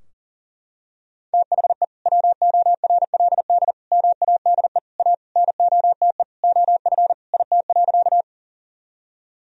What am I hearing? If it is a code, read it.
Morse code, heard as a keyed tone, about 37 wpm, 700 Hz. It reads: THE WORLD MADE A NOTE OF IT.
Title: Sentences from Top 1000 Words - Review Part 2 - 40wpm